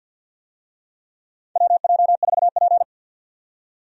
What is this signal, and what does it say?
Morse code, heard as a keyed tone, about 50 wpm, 700 Hz. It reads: WJ4P
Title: USA Call Signs - 4 Characters - 50wpm